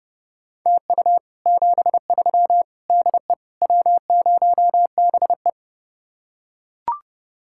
Morse code, heard as a keyed tone, about 30 wpm, 700 Hz. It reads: TU 73 DE W0BE